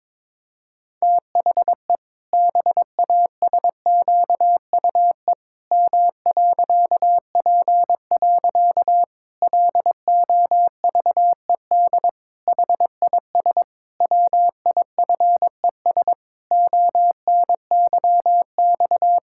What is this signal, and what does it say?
Morse code, heard as a keyed tone, about 22 wpm, 700 Hz. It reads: THE BASQUE M.P. LOVED HIS WIFES ONYX
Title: Pangram Challenge v2 - 22wpm